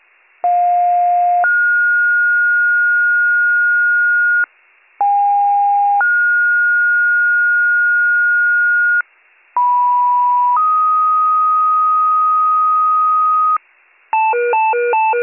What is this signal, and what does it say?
Two-tone page, heard as a one-second tone, then a three-second tone, about 710 then 1430 Hz, about 810 then 1430 Hz, about 980 then 1250 Hz, with noise. It is heard from 0.4 to 13.6 s.